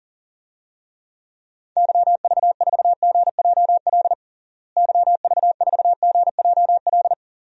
Morse code, heard as a keyed tone, about 40 wpm, 700 Hz. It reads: YV4GJL YV4GJL
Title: DXCC-100 Call Signs - 6 Characters - Repeated 2x - 40wpm